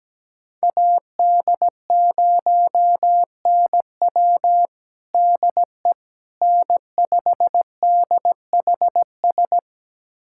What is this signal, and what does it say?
Morse code, heard as a keyed tone, about 17 wpm, 700 Hz. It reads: AD0NW DE N5DHS
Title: USA Callsign DE Callsign - 5 Characters - 17wpm